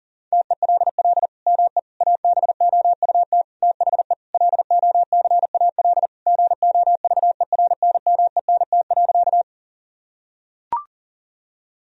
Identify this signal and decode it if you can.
Morse code, heard as a keyed tone, about 40 wpm, 700 Hz. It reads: TELL ME ABOUT THE LOCAL GOVERNMENT.
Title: Sentences from Top 1000 Words - Review Part 3 - 40wpm